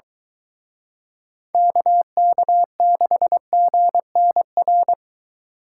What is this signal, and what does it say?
Morse code, heard as a keyed tone, about 23 wpm, 700 Hz. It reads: KK6GNR